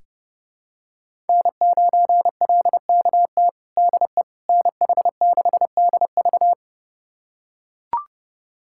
Morse code, heard as a keyed tone, about 32 wpm, 700 Hz. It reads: N9LKT DE NH6DV
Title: USA Callsign DE Callsign - 5 Characters - 30wpm